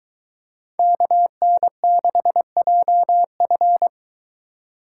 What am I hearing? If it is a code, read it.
Morse code, heard as a keyed tone, about 23 wpm, 700 Hz. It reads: KN6JF